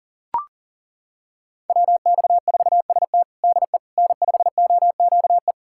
Morse code, heard as a keyed tone, about 40 wpm, 700 Hz. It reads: WX4ST DE N5OQE
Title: USA Callsign DE Callsign 40wpm